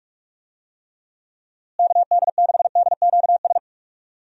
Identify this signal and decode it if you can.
Morse code, heard as a keyed tone, about 46 wpm, 700 Hz. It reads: KD6DQS
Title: USA Call Signs - 6 Characters - Repeated 2x - 45wpm